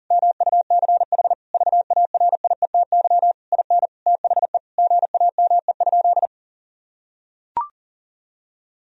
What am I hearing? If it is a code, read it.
Morse code, heard as a keyed tone, about 37 wpm, 700 Hz. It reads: MUCH VARIETY IN THE GAME?
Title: Sentences from Top 1000 Words - Review Part 4 - 40wpm